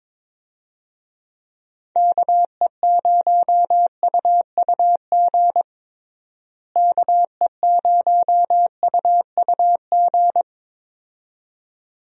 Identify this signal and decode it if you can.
Morse code, heard as a keyed tone, about 22 wpm, 700 Hz. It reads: KE0UUG KE0UUG
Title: USA Call Signs - 6 Characters - Repeated 2x - 22wpm